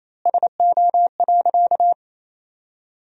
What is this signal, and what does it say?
Morse code, heard as a keyed tone, about 28 wpm, 700 Hz. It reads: SO.
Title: 4x Word-Spacing - Sentences from Top 200 Words - 28wpm